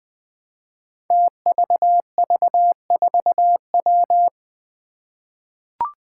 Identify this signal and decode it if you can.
Morse code, heard as a keyed tone, about 20 wpm, 700 Hz. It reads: TVV4W
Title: Five Challenging Letter-Number Combinations 20wpm